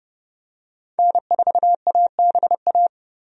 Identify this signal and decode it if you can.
Morse code, heard as a keyed tone, about 30 wpm, 700 Hz. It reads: N4ABA
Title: USA Call Signs - 5 Characters - 30wpm